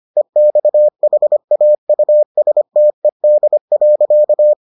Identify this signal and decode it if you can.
Morse code, heard as a keyed tone, about 25 wpm, 600 Hz. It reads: EXHAUSTED.